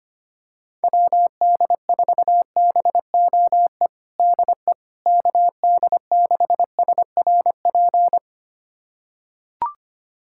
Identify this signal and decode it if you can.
Morse code, heard as a keyed tone, about 25 wpm, 700 Hz. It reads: WD4BOE DE KD6SRP
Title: USA Callsign DE Callsign 25wpm